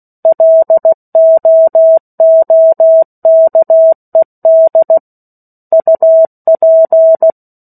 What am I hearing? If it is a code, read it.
Morse code, heard as a keyed tone, about 16 wpm, 650 Hz. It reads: LOOKED UP
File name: 03-Chapter-03 16 WPM